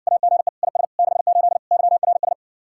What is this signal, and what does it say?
Morse code, heard as a keyed tone, about 60 wpm, 700 Hz. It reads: AGE IS 68 YRS